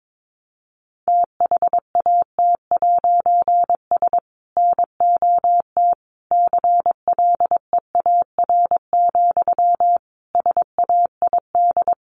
Morse code, heard as a keyed tone, about 22 wpm, 700 Hz. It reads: THAT'S NOT CLEAR, SAID
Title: A Sound Of Thunder 22wpm